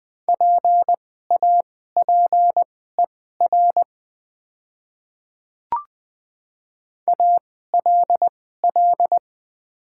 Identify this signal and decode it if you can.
Morse code, heard as a keyed tone, about 20 wpm, spaced as for 15 wpm, 700 Hz. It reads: PAPER ALL
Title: 2x Character Spacing - Top 300 Words - 20wpm